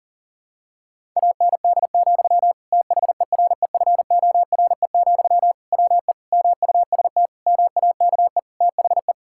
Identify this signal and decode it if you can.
Morse code, heard as a keyed tone, about 40 wpm, 700 Hz. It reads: AND, THEREFORE, WE MUST MAKE THE
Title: Sentences from Top 1000 Words - Review Part 5 - 40wpm